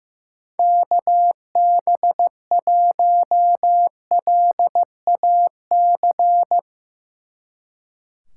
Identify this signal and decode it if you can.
Morse code, heard as a keyed tone, about 15 wpm, 700 Hz. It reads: KB1LAC
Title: Speed Racing - USA 2x3 Call Signs 15wpm